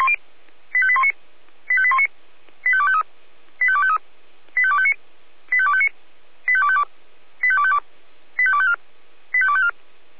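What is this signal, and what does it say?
Five-tone selective call (ZVEI), heard as quick runs of five tones, tones of about 70 ms. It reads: 85819 85819 85253 85253 85259 85259 85252 85252 85254 85254
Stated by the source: ZVEI five-tone selective call received by SDR.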